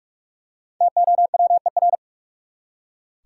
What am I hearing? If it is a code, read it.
Morse code, heard as a keyed tone, about 41 wpm, 700 Hz. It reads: TOWER
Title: Five Letter Words - 45wpm